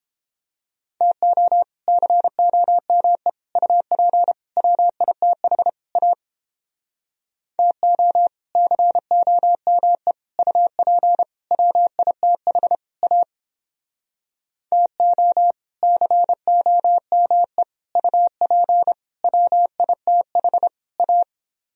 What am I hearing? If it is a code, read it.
Morse code, heard as a keyed tone, about 29 wpm, 700 Hz. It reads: TO COME UP WITH A TO COME UP WITH A TO COME UP WITH A